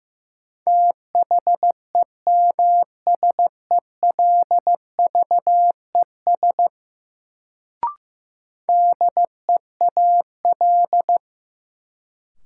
Morse code, heard as a keyed tone, about 15 wpm, 700 Hz. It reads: THEMSELVES DEAL
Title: Sentences from Top 800 Words - New Word Drill - 15wpm